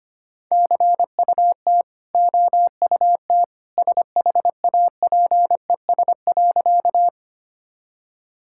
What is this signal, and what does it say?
Morse code, heard as a keyed tone, about 25 wpm, 700 Hz. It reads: CUT OUT SHAPES.
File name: Easy Sentences with Hints - 25wpm